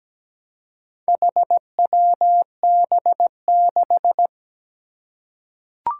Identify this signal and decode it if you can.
Morse code, heard as a keyed tone, about 17 wpm, 700 Hz. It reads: HWB6
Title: Four Challenging Letter-Number Combinations 17wpm